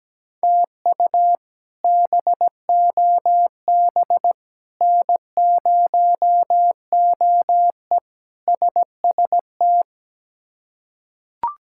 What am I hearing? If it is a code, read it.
Morse code, heard as a keyed tone, about 17 wpm, 700 Hz. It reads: TU BOB N0OE SST
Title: K1USN SST QSO Practice - 17wpm